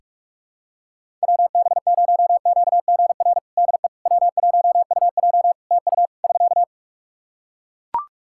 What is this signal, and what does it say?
Morse code, heard as a keyed tone, about 42 wpm, 700 Hz. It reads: WB0QGR DE W1UJ TU <SK>
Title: Practice QSO - Exchange 4 - Ending - 45wpm